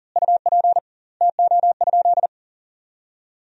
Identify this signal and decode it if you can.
Morse code, heard as a keyed tone, about 40 wpm, 700 Hz. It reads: UP TO?
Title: 2x Word-Spacing - Sentences from Top 100 Words - 40wpm